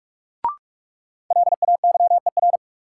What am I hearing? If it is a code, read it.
Morse code, heard as a keyed tone, about 46 wpm, 700 Hz. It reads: LAYER
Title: Sentences from Top 1600 Words - New Word Drill Alt - 45wpm